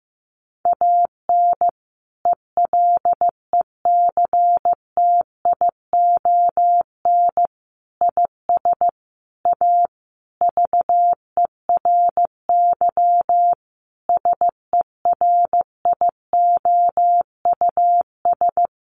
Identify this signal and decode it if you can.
Morse code, heard as a keyed tone, about 15 wpm, 700 Hz. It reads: AN ELECTION IS A VERY SERIOUS
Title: Sentences from Top 1500 Words 15wpm